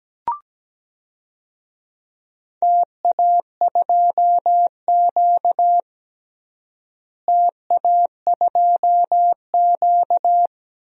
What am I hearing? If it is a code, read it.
Morse code, heard as a keyed tone, about 17 wpm, 700 Hz. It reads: TA2Q TA2Q
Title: DXCC-100 Call Signs - 4 Characters - Repeated 2x - 17wpm